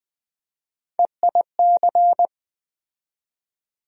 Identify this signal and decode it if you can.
Morse code, heard as a keyed tone, about 20 wpm, 700 Hz. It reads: EIC
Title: USA Call Sign Postfixes - 3 Letters 20wpm